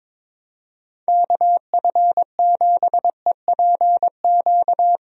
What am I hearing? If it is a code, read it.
Morse code, heard as a keyed tone, about 22 wpm, 700 Hz. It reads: KF7EPQ
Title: USA Call Signs - 6 Characters - Repeated 2x - 22wpm